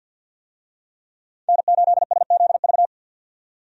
Morse code, heard as a keyed tone, about 46 wpm, 700 Hz. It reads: N8SZV
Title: USA Call Signs - 5 Characters - 50wpm